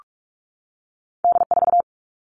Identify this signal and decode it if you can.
Morse code, heard as a keyed tone, about 46 wpm, 700 Hz. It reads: D4
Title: All Cut Numbers - 2-Digit 45wpm